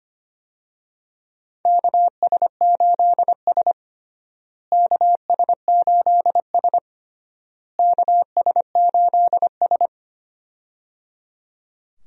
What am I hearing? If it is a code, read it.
Morse code, heard as a keyed tone, about 25 wpm, 700 Hz. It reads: KS8S KS8S KS8S
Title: USA Call Signs - 4 Characters - Repeated 3x - 25wpm